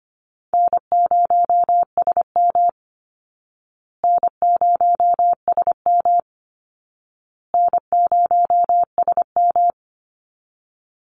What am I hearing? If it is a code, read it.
Morse code, heard as a keyed tone, about 25 wpm, 700 Hz. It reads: N0SM N0SM N0SM
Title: Easy Call Signs Repeated 3x - 25wpm